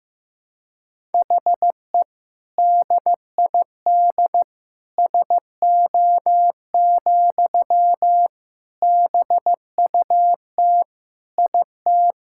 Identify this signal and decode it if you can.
Morse code, heard as a keyed tone, about 15 wpm, 700 Hz. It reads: HE DID SO, BUT IT
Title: Sentences from Top 1000 Words - Review Part 5 - 15wpm